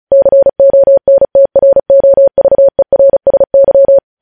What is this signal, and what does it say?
Morse code, heard as a keyed tone, about 36 wpm, 550 Hz. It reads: CONTROVERSY
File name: podcast / News headlines-1x-35x35